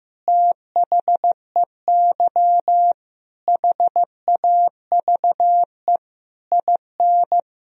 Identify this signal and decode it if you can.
Morse code, heard as a keyed tone, about 15 wpm, 700 Hz. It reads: THEY HAVE IN
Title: Sentences from Top 1000 Words - Review Part 2 - 15wpm